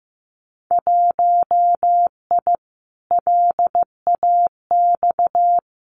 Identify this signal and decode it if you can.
Morse code, heard as a keyed tone, about 15 wpm, 700 Hz. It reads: 1I LAX
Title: Winter Field Day Exchanges Encore 15wpm